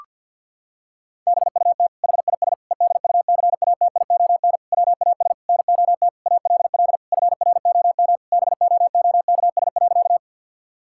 Morse code, heard as a keyed tone, about 52 wpm, 700 Hz. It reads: BUT HIS EDUCATION WAS NOT ALL FROM BOOKS.
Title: Sentences from Top 1000 Words - Review Part 6 - 50wpm